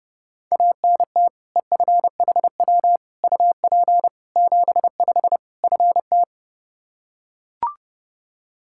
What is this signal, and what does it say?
Morse code, heard as a keyed tone, about 32 wpm, 700 Hz. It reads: ANT EFHW UP 75 FT
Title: K1USN SST QSO Practice 30wpm